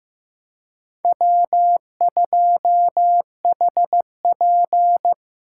Morse code, heard as a keyed tone, about 18 wpm, 700 Hz. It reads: W2HP
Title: Speed Racing - USA Call Signs 15wpm